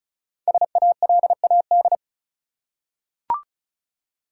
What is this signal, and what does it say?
Morse code, heard as a keyed tone, about 36 wpm, 700 Hz. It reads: SALAD
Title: Five Letter Words - 35wpm